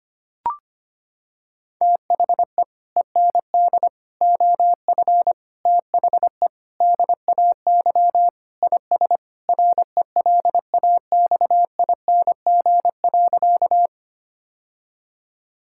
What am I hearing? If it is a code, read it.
Morse code, heard as a keyed tone, about 25 wpm, 700 Hz. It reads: THE END OF THE DAY IS RELAXING.
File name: practice / 25wpm / Sentences with 3-Word Phrases - 25wpm